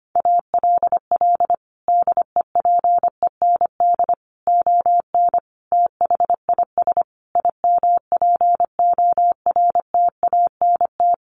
Morse code, heard as a keyed tone, about 25 wpm, 700 Hz. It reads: ALL DEPEND ON THIS IMPORTANT
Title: Sentences from Top 1300 Words 25wpm